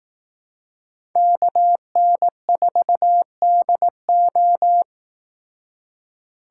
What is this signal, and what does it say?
Morse code, heard as a keyed tone, about 18 wpm, 700 Hz. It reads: KN4DO
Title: Speed Racing - USA 2x2 Call Signs 15wpm